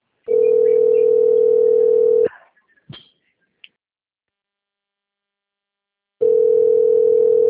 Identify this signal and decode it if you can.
Ringback tone (North America), 440 + 480 Hz.